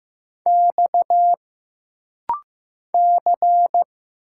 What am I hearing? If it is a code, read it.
Morse code, heard as a keyed tone, about 15 wpm, 700 Hz. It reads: X C
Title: Single Letters - Rapid-Fire 15wpm